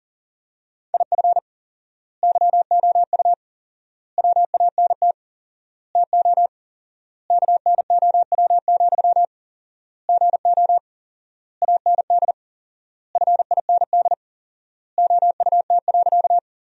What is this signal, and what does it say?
Morse code, heard as a keyed tone, about 40 wpm, 700 Hz. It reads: IF YOU WANT TO KNOW, GO AND FIND OUT.
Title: 4x Word-Spacing - Sentences from Top 200 Words - 40wpm